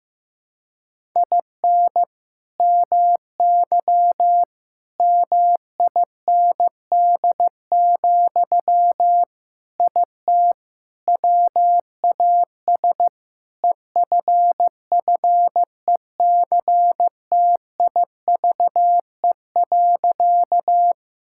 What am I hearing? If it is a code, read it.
Morse code, heard as a keyed tone, about 16 wpm, 700 Hz. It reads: IN MY MIND, IT WAS EFFECTIVE.
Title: Sentences from Top 1000 Words - Review Part 6 - 15wpm